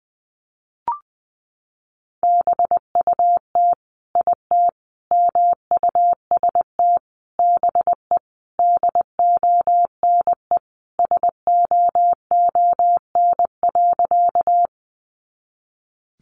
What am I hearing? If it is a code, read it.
Morse code, heard as a keyed tone, about 20 wpm, 700 Hz. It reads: BUT IT MUST BE DONE SOON.
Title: Sentences from Top 500 Words 20wpm